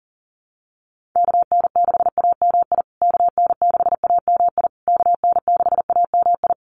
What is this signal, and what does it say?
Morse code, heard as a keyed tone, about 39 wpm, 700 Hz. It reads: KN6AMI KN6AMI KN6AMI
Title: USA Call Signs Repeated 3x 40wpm